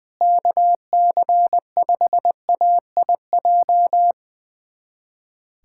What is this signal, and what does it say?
Morse code, heard as a keyed tone, about 21 wpm, 700 Hz. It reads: KC5AIJ
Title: USA Callsign DE Callsign - 6 Characters - 20wpm